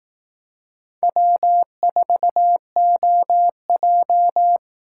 Morse code, heard as a keyed tone, about 18 wpm, 700 Hz. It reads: W4OJ
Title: Speed Racing - USA 1x2 Call Signs 15wpm